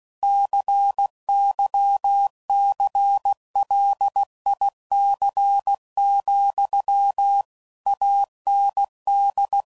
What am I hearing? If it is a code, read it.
Morse code, heard as a keyed tone, about 16 wpm, 800 Hz. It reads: CYCLIC, AND
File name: CW Code Groups @15WPM/Technical Subjects